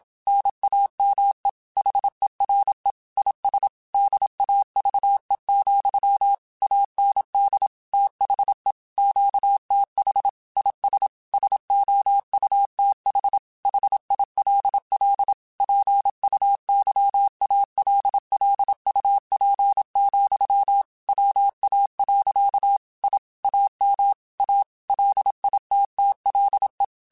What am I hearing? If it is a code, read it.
Morse code, heard as a keyed tone, about 27 wpm, 800 Hz. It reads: NAME HERE IS DAVE, AND THE QTH IS SOUTH HILL PUYALLUP, WA. I AM A LITTLE
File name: QSOs / QSOs-mp3s / kc7sac@25